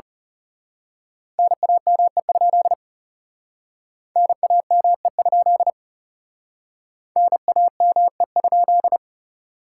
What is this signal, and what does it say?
Morse code, heard as a keyed tone, about 34 wpm, 700 Hz. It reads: NAME? NAME? NAME?
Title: Speed-Racing Common QSO Phrases - 30wpm